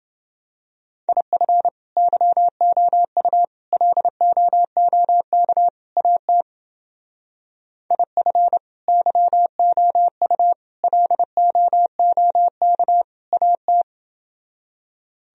Morse code, heard as a keyed tone, about 28 wpm, 700 Hz. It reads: IF YOU LOOK AT IF YOU LOOK AT